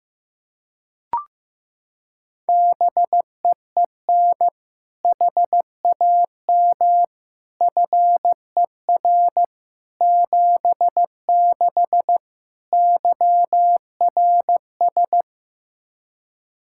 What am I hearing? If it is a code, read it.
Morse code, heard as a keyed tone, about 15 wpm, 700 Hz. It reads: BEEN HAM FER 76 YRS
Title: Practice QSO - Exchange 3 - More Info - 15wpm